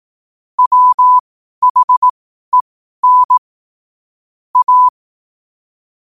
Morse code, heard as a keyed tone, about 18 wpm, spaced as for 12 wpm, 1000 Hz. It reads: WHEN A
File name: qotdfiles / 12wpm